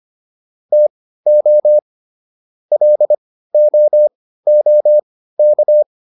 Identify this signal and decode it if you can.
Morse code, heard as a keyed tone, about 25 wpm, spaced as for 15 wpm, 600 Hz. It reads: TO LOOK